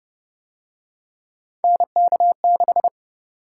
Morse code, heard as a keyed tone, about 31 wpm, 700 Hz. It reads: NK6